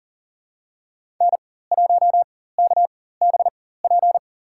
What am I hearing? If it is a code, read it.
Morse code, heard as a keyed tone, about 40 wpm, spaced as for 19 wpm, 700 Hz. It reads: N1KBP
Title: USA Call Signs - 5 Characters - 4x Character Spacing - 40wpm